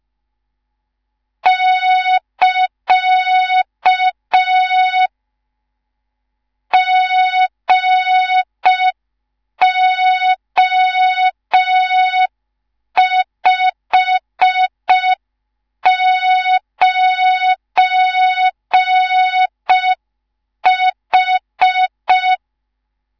Morse code, heard as a keyed tone, about 5 wpm, 3700 Hz. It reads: <KA> GO59H
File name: Morze-HA2MN / hang / 19 GYAKORLAT